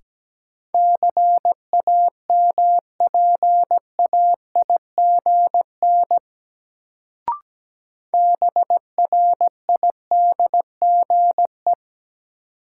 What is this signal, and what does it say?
Morse code, heard as a keyed tone, about 17 wpm, 700 Hz. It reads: CAMPAIGN BRIDGE
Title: Sentences from Top 1200 Words - New Word Drill - 17wpm